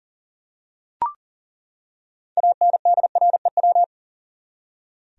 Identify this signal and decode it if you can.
Morse code, heard as a keyed tone, about 40 wpm, 700 Hz. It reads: ANDREW
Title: USA Common Names - 40wpm